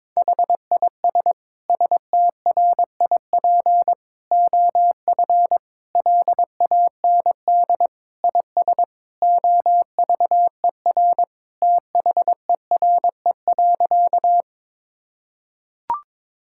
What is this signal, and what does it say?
Morse code, heard as a keyed tone, about 22 wpm, 700 Hz. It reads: HIS STRIP OF LAND IS OVER THERE.
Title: Sentences from Top 1600 Words 22wpm